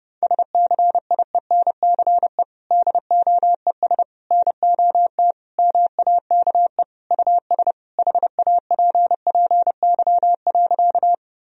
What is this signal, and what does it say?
Morse code, heard as a keyed tone, about 32 wpm, 700 Hz. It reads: SCIENCE DOES NOT MAKE US HAPPY.
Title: Sentences from Top 1000 Words - Review Part 2 - 30wpm